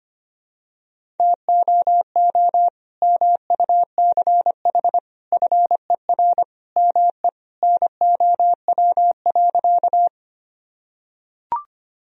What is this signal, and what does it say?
Morse code, heard as a keyed tone, about 25 wpm, 700 Hz. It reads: TOO MUCH FER ME NOW.
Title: Sentences from Top 200 Words and Abbreviations 25wpm